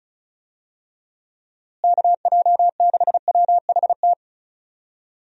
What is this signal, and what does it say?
Morse code, heard as a keyed tone, about 36 wpm, 700 Hz. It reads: KJ6WHT
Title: USA Call Signs - 6 Characters - Repeated 3x - 35wpm